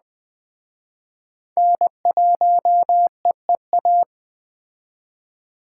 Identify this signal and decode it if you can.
Morse code, heard as a keyed tone, about 20 wpm, 700 Hz. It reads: N1EEA
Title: Speed Racing - USA 1x3 Call Signs 17wpm